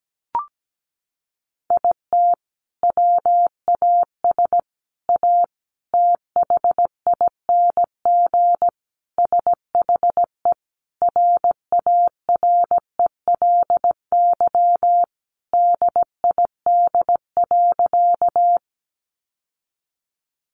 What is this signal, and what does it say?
Morse code, heard as a keyed tone, about 17 wpm, 700 Hz. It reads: IT WAS A THING SHE RARELY DID.
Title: Sentences from Top 1400 Words 17wpm